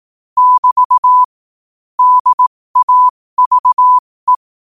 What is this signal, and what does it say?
Morse code, heard as a keyed tone, about 18 wpm, spaced as for 15 wpm, 1000 Hz. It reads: <BT> DAVE
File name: qotdfiles / 15wpm